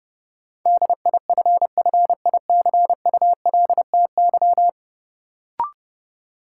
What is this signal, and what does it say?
Morse code, heard as a keyed tone, about 29 wpm, 700 Hz. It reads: DIFFICULTY
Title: Sentences from Top 1500 Words - New Word Drill - 30wpm